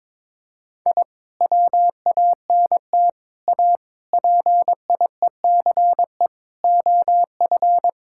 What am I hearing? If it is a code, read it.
Morse code, heard as a keyed tone, about 22 wpm, 700 Hz. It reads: I WANT A PIECE OF
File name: practice / 22wpm / Sentences with 3-Word Phrases - 22wpm